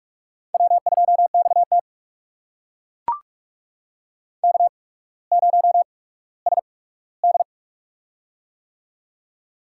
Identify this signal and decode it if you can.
Morse code, heard as a keyed tone, about 42 wpm, spaced as for 19 wpm, 690 Hz. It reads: W2XT K0SD
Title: USA Call Signs - 4 Characters - 8x Character Spacing - 45wpm